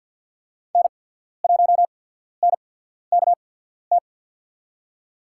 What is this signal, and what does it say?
Morse code, heard as a keyed tone, about 47 wpm, spaced as for 13 wpm, 700 Hz. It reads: N1NKT
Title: USA Call Signs - 5 Characters - 8x Character Spacing - 50wpm